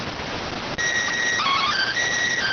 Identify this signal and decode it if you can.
SSTV (slow-scan television), PD 290.